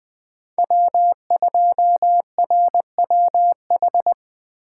Morse code, heard as a keyed tone, about 20 wpm, 700 Hz. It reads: W2RWH
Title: USA Call Signs - 5 Characters - 8x Character Spacing - 20wpm